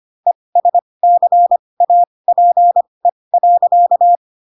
Morse code, heard as a keyed tone, about 25 wpm, spaced as for 20 wpm, 700 Hz. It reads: ESCAPE.
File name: Pinocchio / Pinocchio ch4 15 25 ik1hge